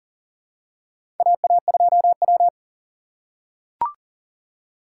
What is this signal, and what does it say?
Morse code, heard as a keyed tone, about 40 wpm, 700 Hz. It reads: AA2W
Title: USA Call Signs - 4 Characters - 3x Character Spacing - 40wpm